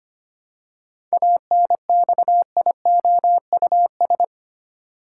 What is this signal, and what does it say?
Morse code, heard as a keyed tone, about 25 wpm, 700 Hz. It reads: ANXIOUS
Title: Seven Letter Words - 25wpm